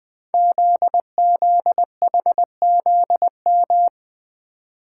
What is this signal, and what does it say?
Morse code, heard as a keyed tone, about 20 wpm, 700 Hz. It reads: ZZHZM
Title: Code Groups - 5 Letters - 20wpm